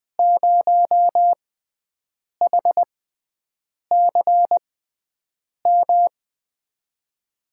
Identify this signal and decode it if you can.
Morse code, heard as a keyed tone, about 20 wpm, spaced as for 7 wpm, 700 Hz. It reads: 0HCM
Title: USA Call Signs - 6x Character Spacing - 20wpm